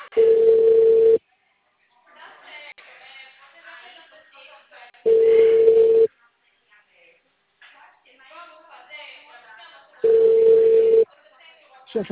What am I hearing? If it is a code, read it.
Ringback tone (China, 450 Hz), with speech.